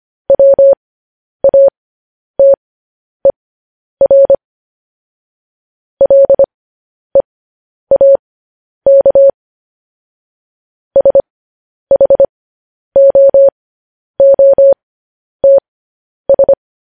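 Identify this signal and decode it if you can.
Morse code, heard as a keyed tone, about 25 wpm, spaced as for 10 wpm, 550 Hz. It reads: WATER LEAK SHOOTS
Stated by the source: Morse news headlines sent at 10 words per minute.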